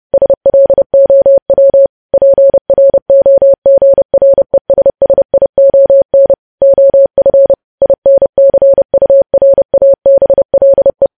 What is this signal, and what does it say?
Morse code, heard as a keyed tone, about 30 wpm, 550 Hz. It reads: SLOW PROGRESSION OF INCURABLE
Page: news headlines in morse code at 30 wpm